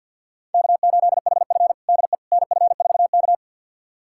Morse code, heard as a keyed tone, about 53 wpm, 700 Hz. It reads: K8HF DE NF4K
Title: USA Callsign DE Callsign - 4 Characters - 50wpm